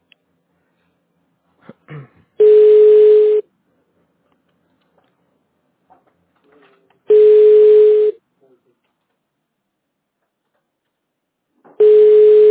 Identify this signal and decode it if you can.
Ringback tone (Europe, 425 Hz), with speech.
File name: 1023. .d2d0fa6a-f630-44af-a675